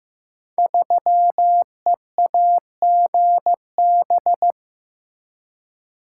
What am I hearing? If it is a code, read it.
Morse code, heard as a keyed tone, about 15 wpm, 700 Hz. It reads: 3EAGB
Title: Code Groups - 5 Letter:Numbers - 15wpm